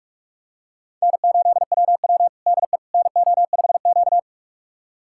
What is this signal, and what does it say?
Morse code, heard as a keyed tone, about 41 wpm, 700 Hz. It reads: N8WW DE NO5Q
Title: USA Callsign DE Callsign - 4 Characters - 45wpm